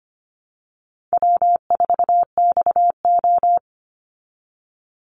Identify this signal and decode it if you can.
Morse code, heard as a keyed tone, about 25 wpm, 700 Hz. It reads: W4XO
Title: Easy Call Signs Repeated 3x - 25wpm